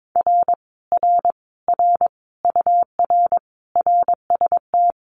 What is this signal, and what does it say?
Morse code, heard as a keyed tone, about 22 wpm, 700 Hz. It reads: R R R UR RST
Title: Common QSO Phrases Encore 22wpm